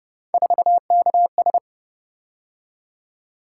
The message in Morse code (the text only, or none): 4KS